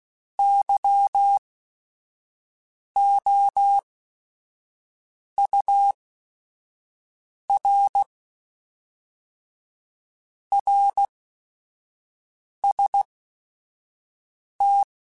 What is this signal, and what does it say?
Morse code, heard as a keyed tone, about 16 wpm, spaced as for 5 wpm, 800 Hz. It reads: YOUR RST
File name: Your RSTs @5